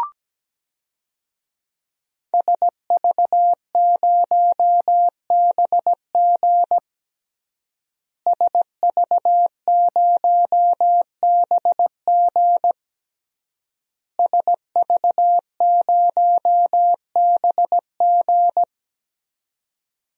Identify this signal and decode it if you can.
Morse code, heard as a keyed tone, about 17 wpm, 700 Hz. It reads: SV0BG SV0BG SV0BG
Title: DXCC-100 Call Signs - 5 Characters - Repeated 3x - 17wpm